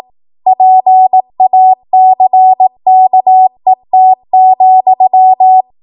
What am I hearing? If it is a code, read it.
Morse code, heard as a keyed tone, about 18 wpm, 750 Hz. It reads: PACKET,
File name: Archive / 18 WPM / 18WPM